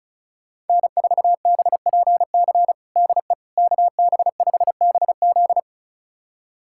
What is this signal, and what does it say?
Morse code, heard as a keyed tone, about 34 wpm, 700 Hz. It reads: N4BPC DE KB5BZ
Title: USA Callsign DE Callsign - 5 Characters - 35wpm